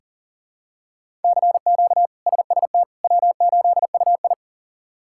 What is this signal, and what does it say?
Morse code, heard as a keyed tone, about 40 wpm, 700 Hz. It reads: CQ SST W8UI
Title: K1USN SST QSO Practice - 40wpm